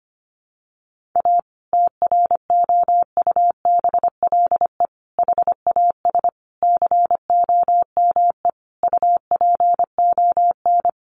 Morse code, heard as a keyed tone, about 25 wpm, 700 Hz. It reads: A TROUBLE HAS COME UPON
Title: Sentences from Top 1200 Words 25wpm